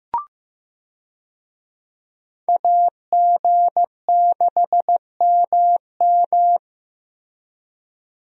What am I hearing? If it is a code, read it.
Morse code, heard as a keyed tone, about 14 wpm, 700 Hz. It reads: AG6MM
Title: USA Call Signs - 5 Characters - Repeated 3x - 15wpm